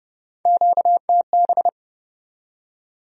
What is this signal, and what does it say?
Morse code, heard as a keyed tone, about 40 wpm, 700 Hz. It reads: QTB